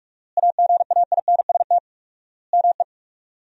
Morse code, heard as a keyed tone, about 46 wpm, 700 Hz. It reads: AGAINST ME
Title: 4x Word-Spacing - Sentences from Top 400 Words - 45wpm